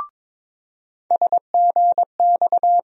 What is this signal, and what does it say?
Morse code, heard as a keyed tone, about 22 wpm, 700 Hz. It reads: SGX
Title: USA Call Sign Postfixes - 3 Letters 22wpm